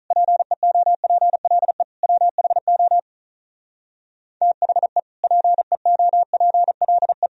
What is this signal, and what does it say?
Morse code, heard as a keyed tone, about 46 wpm, 700 Hz. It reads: PEOPLE WHO THE PEOPLE